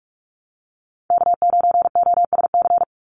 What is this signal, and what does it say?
Morse code, heard as a keyed tone, about 46 wpm, 700 Hz. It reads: K9OSC
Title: NA SOTA Chaser Callsigns 45wpm